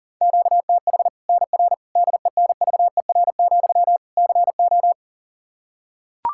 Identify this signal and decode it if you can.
Morse code, heard as a keyed tone, about 40 wpm, 700 Hz. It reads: QTH NR DENVER, CO
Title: Practice QSO - Exchange 1 - Beginning - 40wpm